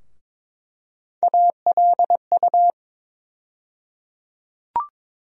Morse code, heard as a keyed tone, about 22 wpm, 700 Hz. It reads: ALU
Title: USA Call Sign Postfixes - 3 Letters 22wpm